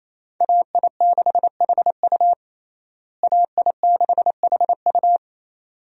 Morse code, heard as a keyed tone, about 28 wpm, 700 Hz. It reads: AI6HU AI6HU
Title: USA Call Signs - 5 Characters - Repeated 2x - 28wpm